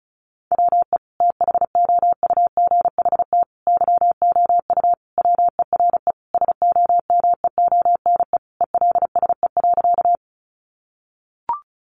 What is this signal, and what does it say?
Morse code, heard as a keyed tone, about 34 wpm, 700 Hz. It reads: WE THOUGHT YOU WERE SOMEONE ELSE.
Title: Sentences from Top 800 Words 35wpm